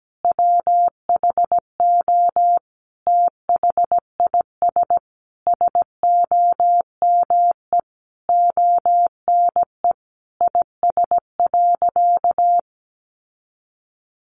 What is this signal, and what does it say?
Morse code, heard as a keyed tone, about 17 wpm, 700 Hz. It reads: WHO THIS SOME ONE IS.
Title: Sentences from Top 1100 Words 17wpm